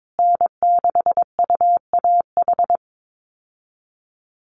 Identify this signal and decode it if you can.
Morse code, heard as a keyed tone, about 22 wpm, 700 Hz. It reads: N6UAH
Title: USA Call Signs Repeated 2x 22wpm